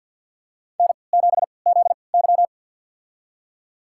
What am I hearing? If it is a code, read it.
Morse code, heard as a keyed tone, about 50 wpm, spaced as for 28 wpm, 700 Hz. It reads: N7ZY
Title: USA Call Signs - 4 Characters - 3x Character Spacing - 50wpm